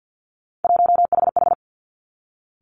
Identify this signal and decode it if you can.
Morse code, heard as a keyed tone, about 50 wpm, 700 Hz. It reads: 1HH
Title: Three Challenging Letter-Number Combinations 50wpm